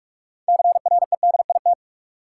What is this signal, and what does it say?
Morse code, heard as a keyed tone, about 42 wpm, 700 Hz. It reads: CREDIT
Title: Sentences from Top 1000 Words - New Word Drill - 45wpm